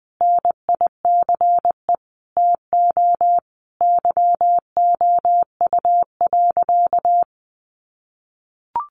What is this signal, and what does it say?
Morse code, heard as a keyed tone, about 20 wpm, 700 Hz. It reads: NICE TO YOU.
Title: Sentences from Top 1100 Words 20wpm